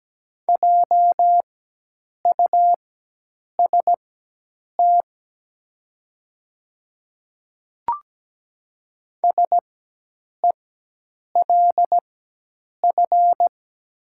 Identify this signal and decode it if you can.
Morse code, heard as a keyed tone, about 17 wpm, spaced as for 8 wpm, 700 Hz. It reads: JUST SELF
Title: 4x Character Spacing - Top 200 Words - 17wpm